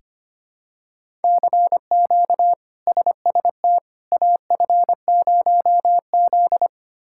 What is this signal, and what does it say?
Morse code, heard as a keyed tone, about 25 wpm, 700 Hz. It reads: CQ SST AF0Z
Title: K1USN SST QSO Practice - 25wpm